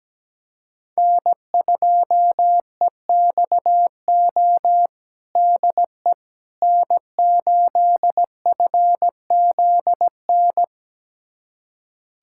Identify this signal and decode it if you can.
Morse code, heard as a keyed tone, about 17 wpm, 700 Hz. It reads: N2EXO DE N8FZN
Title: USA Callsign DE Callsign - 5 Characters - 17wpm